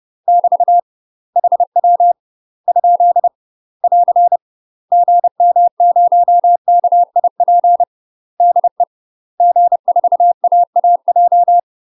Morse code, heard as a keyed tone, about 30 wpm, 700 Hz. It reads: <BT> HW ? <AR> GM0KIP DE G4AAJ
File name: UK MASTER QSOs TRAINING / 30WPM / UK MASTER QSOs 0052